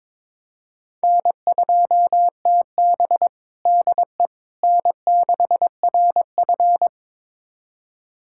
Morse code, heard as a keyed tone, about 22 wpm, 700 Hz. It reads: N2TB DE N6RF
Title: USA Callsign DE Callsign - 4 Characters - 22wpm